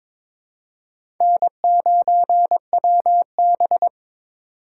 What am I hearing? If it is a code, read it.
Morse code, heard as a keyed tone, about 22 wpm, 700 Hz. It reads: N9WB
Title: USA Call Signs - 4 Characters - 4x Character Spacing - 22wpm